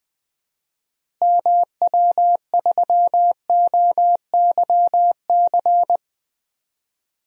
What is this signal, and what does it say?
Morse code, heard as a keyed tone, about 20 wpm, 700 Hz. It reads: MW3OYC
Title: DXCC-100 Call Signs - 6 Characters - Repeated 2x - 20wpm